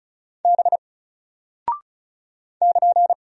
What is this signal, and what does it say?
Morse code, heard as a keyed tone, about 36 wpm, 700 Hz. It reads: B <KN>
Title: Single Character (Letter:Number:Punctuation:Prosign) Alt 35wpm